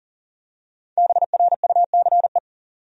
Morse code, heard as a keyed tone, about 39 wpm, 700 Hz. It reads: BRUCE
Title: USA Common Names - 40wpm